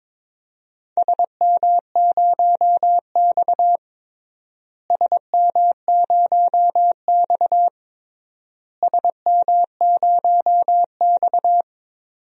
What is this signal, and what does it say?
Morse code, heard as a keyed tone, about 21 wpm, 700 Hz. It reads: SM0X SM0X SM0X
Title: DXCC-100 Call Signs - 4 Characters - Repeated 3x - 22wpm